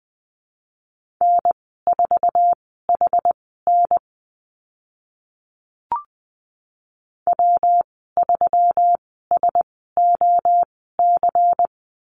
Morse code, heard as a keyed tone, about 20 wpm, spaced as for 14 wpm, 700 Hz. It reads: N4HN W3SOC
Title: 140 Lesson 23g - Letter f - Call Signs 20wpm